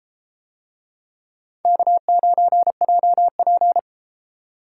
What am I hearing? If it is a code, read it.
Morse code, heard as a keyed tone, about 30 wpm, 700 Hz. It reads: K9JP